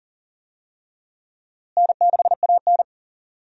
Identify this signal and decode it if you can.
Morse code, heard as a keyed tone, about 40 wpm, 700 Hz. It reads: N6AN
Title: USA Call Signs - 4 Characters - Repeated 3x - 40wpm